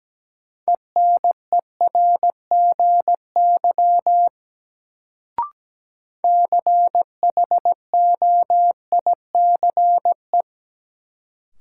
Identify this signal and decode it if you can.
Morse code, heard as a keyed tone, about 17 wpm, 700 Hz. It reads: ENERGY CHOICE
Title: Sentences from Top 900 Words - New Word Drill - 17wpm